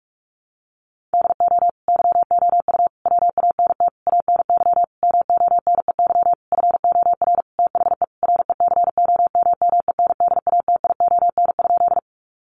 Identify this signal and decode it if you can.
Morse code, heard as a keyed tone, about 44 wpm, 700 Hz. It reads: DO YOU WANT ANY MONEY FOR THE RECOMMENDATION?
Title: Sentences from Top 1400 Words 45wpm